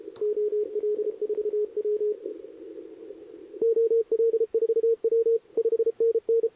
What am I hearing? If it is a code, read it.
Morse code, heard as a keyed tone, about 31 wpm, 440 Hz, with radio noise. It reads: OL4W OL4W 5NN
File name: CQWWCW2023 / SP